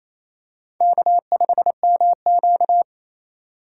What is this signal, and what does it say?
Morse code, heard as a keyed tone, about 27 wpm, 700 Hz. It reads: K5MQ
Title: USA Call Signs - 2x Character Spacing - 28wpm